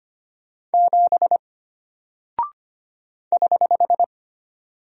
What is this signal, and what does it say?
Morse code, heard as a keyed tone, about 25 wpm, 700 Hz. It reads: 7 <HH>
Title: Single Character (Letter:Number:Punctuation:Prosign) Alt 25wpm